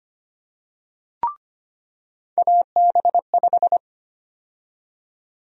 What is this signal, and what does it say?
Morse code, heard as a keyed tone, about 25 wpm, 700 Hz. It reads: AB5
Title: USA Call Sign Prefixes - 25wpm